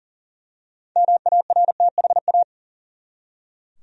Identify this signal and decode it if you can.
Morse code, heard as a keyed tone, about 40 wpm, 700 Hz. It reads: MARTHA
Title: USA Common Names - 40wpm